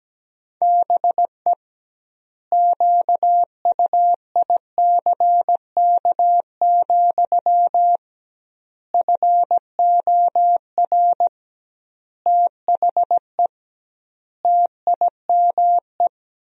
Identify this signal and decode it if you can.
Morse code, heard as a keyed tone, about 17 wpm, 700 Hz. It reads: BE QUICK, FOR THE TIME
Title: 2x Word-Spacing - Sentences from Top 500 Words - 17wpm